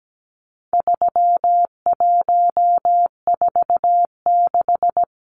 Morse code, heard as a keyed tone, about 17 wpm, 700 Hz. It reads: 3146
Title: Numbers - 4-Digit 17wpm